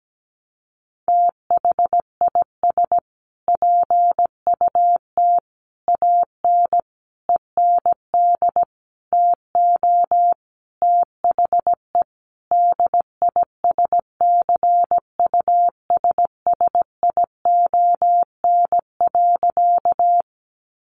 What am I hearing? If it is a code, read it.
Morse code, heard as a keyed tone, about 17 wpm, 700 Hz. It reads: THIS PUT AN END TO THE DISCUSSION.
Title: Sentences from Top 1100 Words 17wpm